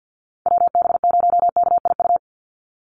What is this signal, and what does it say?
Morse code, heard as a keyed tone, about 49 wpm, 700 Hz. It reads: WB0KIU
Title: NA SOTA Chaser Callsigns 50wpm